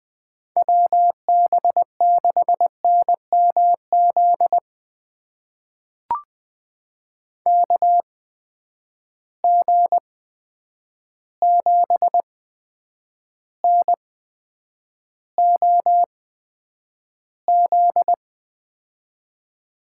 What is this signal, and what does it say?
Morse code, heard as a keyed tone, about 20 wpm, spaced as for 9 wpm, 700 Hz. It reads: WB6NMZ KG7NOZ
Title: USA Call Signs - 6 Characters - 8x Character Spacing - 20wpm